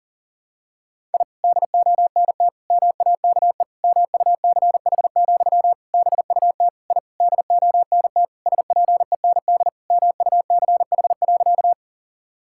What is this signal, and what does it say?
Morse code, heard as a keyed tone, about 40 wpm, 700 Hz. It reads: I DONT MAKE MUCH, BUT I DONT SPEND MUCH.
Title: Sentences from Top 1000 Words - Review Part 5 - 40wpm